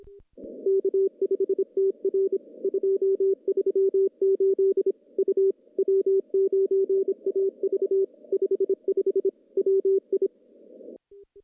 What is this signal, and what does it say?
Morse code, heard as a keyed tone, about 24 wpm, 390 Hz. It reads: K5TR 238 U W9AV 55 WI
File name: contests / CW-SS-k5tr- -0400